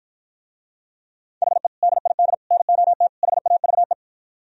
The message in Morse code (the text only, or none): HE DID NOT HAVE